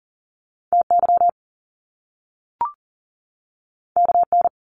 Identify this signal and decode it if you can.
Morse code, heard as a keyed tone, about 42 wpm, 700 Hz. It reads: TY KN
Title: Top Two Letter Combinations in Words 40wpm